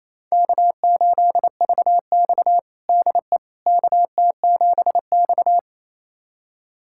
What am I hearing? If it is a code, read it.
Morse code, heard as a keyed tone, about 28 wpm, 700 Hz. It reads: K8VX DE KT7X
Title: USA Callsign DE Callsign - 4 Characters - 28wpm